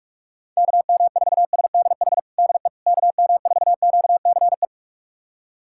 Morse code, heard as a keyed tone, about 46 wpm, 700 Hz. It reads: KM4SDH DE KM4QCE
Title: USA Callsign DE Callsign 45wpm